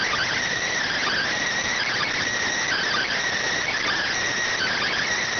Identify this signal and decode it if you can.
Emergency Alert System (SAME) tones, heard as harsh buzzing bursts. It is heard from 0.9 to 4.9 s.